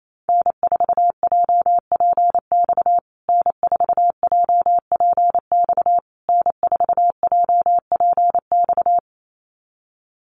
Morse code, heard as a keyed tone, about 28 wpm, 700 Hz. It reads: N4JPX N4JPX N4JPX
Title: USA Call Signs Repeated 3x 28wpm